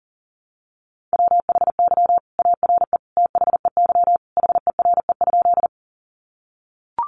Morse code, heard as a keyed tone, about 40 wpm, 700 Hz. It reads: WHY ARE THEY HERE?
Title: Sentences from Top 200 Words 40wpm